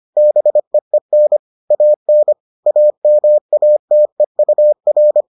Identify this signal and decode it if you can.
Morse code, heard as a keyed tone, about 25 wpm, 600 Hz. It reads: BEEN AN AMATEUR